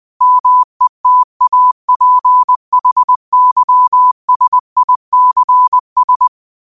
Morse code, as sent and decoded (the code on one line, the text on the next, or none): -- . - .- .--. .... -.-- ... .. -.-. ...
METAPHYSICS